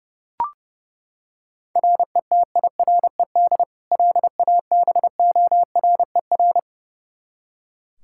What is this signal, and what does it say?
Morse code, heard as a keyed tone, about 30 wpm, 700 Hz. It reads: RETIRED LABORER
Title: Practice QSO - Exchange 3 - More Info - 30wpm